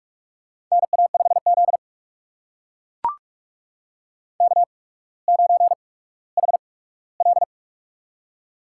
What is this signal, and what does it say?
Morse code, heard as a keyed tone, about 44 wpm, spaced as for 19 wpm, 700 Hz. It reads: NA5Z K9HL
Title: USA Call Signs - 4 Characters - 8x Character Spacing - 45wpm